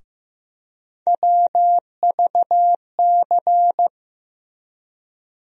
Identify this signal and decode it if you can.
Morse code, heard as a keyed tone, about 15 wpm, 700 Hz. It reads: WVC
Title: USA Call Sign Postfixes - 3 Letters 15wpm